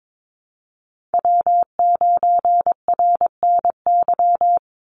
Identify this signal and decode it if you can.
Morse code, heard as a keyed tone, about 22 wpm, 700 Hz. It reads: W9RNY
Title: Call Signs - 22wpm